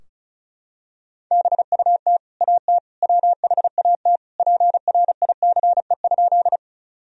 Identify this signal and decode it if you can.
Morse code, heard as a keyed tone, about 34 wpm, 700 Hz. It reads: BUT AT WHAT PRICE?
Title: Sentences from Top 1000 Words - Review Part 4 - 35wpm